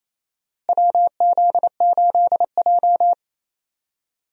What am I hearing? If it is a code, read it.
Morse code, heard as a keyed tone, about 28 wpm, 700 Hz. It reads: WZ8J
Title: USA Call Signs - 4 Characters - Repeated 2x - 28wpm